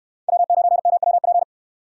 Morse code, heard as a keyed tone, about 69 wpm, 700 Hz. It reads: W9MWP